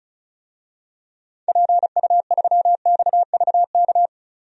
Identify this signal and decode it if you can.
Morse code, heard as a keyed tone, about 34 wpm, 700 Hz. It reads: PU3XVK
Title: DXCC-100 Call Signs - 6 Characters - 35wpm